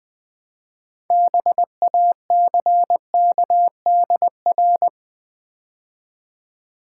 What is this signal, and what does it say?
Morse code, heard as a keyed tone, about 21 wpm, 700 Hz. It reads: BACKDR
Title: Build A Word - 2 Letters At A Time - 8-Letter Words - 20wpm